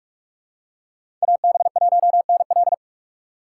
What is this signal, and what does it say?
Morse code, heard as a keyed tone, about 43 wpm, 700 Hz. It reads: AB1NL
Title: USA Call Signs - 2x Character Spacing - 45wpm